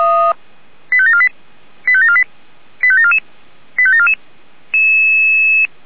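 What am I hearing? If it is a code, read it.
Five-tone selective call (ZVEI), heard as quick runs of five tones, tones of about 70 ms. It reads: 85748 85748 85744 85744
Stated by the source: Five-tone ZVEI selective call received by SDR.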